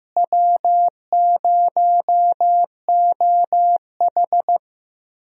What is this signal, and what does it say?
Morse code, heard as a keyed tone, about 16 wpm, 700 Hz. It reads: W0OH
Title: USA Call Signs - 4 Characters - Repeated 3x - 15wpm